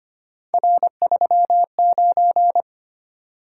Morse code, heard as a keyed tone, about 25 wpm, 700 Hz. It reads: R39
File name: Three Letter-Number Combinations - 25wpm